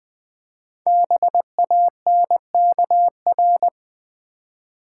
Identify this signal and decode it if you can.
Morse code, heard as a keyed tone, about 20 wpm, 700 Hz. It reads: BANKR
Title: Build A Word - 5 Letters At A Time - 10-Letter Words - 20wpm